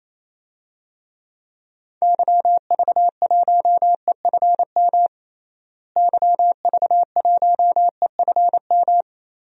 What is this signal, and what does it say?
Morse code, heard as a keyed tone, about 27 wpm, 700 Hz. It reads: YV1EFM YV1EFM
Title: DXCC-100 Call Signs - 6 Characters - Repeated 3x - 28wpm